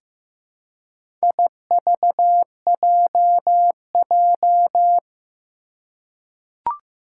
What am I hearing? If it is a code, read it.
Morse code, heard as a keyed tone, about 15 wpm, 700 Hz. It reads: IVJJ
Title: Four Challenging Letter-Number Combinations 15wpm